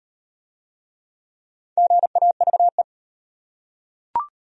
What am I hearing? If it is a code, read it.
Morse code, heard as a keyed tone, about 37 wpm, 700 Hz. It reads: GAVE